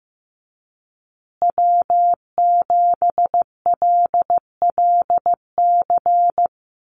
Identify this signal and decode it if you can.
Morse code, heard as a keyed tone, about 15 wpm, 700 Hz. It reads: W7LLC
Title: USA Call Signs Repeated 2x 15wpm